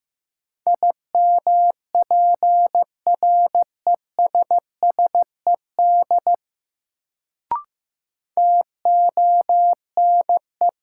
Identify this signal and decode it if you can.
Morse code, heard as a keyed tone, about 15 wpm, 700 Hz. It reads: IMPRESSED TONE
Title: Sentences from Top 1600 Words - New Word Drill Alt - 15wpm